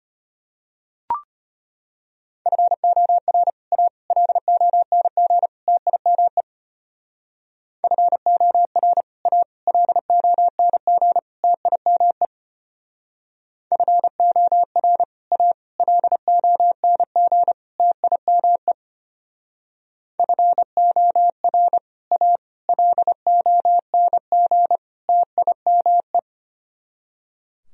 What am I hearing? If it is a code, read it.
Morse code, heard as a keyed tone, about 32 wpm, 700 Hz. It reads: FOR A LONG TIME FOR A LONG TIME FOR A LONG TIME FOR A LONG TIME